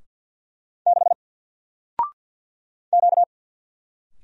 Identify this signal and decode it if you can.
Morse code, heard as a keyed tone, about 52 wpm, 700 Hz. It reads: 6 Q
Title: Single Letter-Number Alt 50wpm